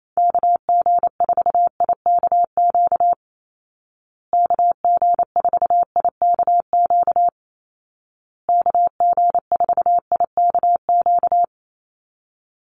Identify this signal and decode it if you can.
Morse code, heard as a keyed tone, about 28 wpm, 700 Hz. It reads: KG4IKQ KG4IKQ KG4IKQ
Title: Call Signs Repeated 2x - 28wpm